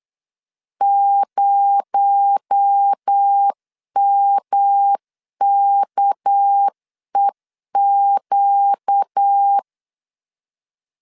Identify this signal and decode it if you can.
Morse code, heard as a keyed tone, about 8 wpm, 800 Hz. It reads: 0MKEQ
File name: LEZIONI / 9 286 8wpm 3425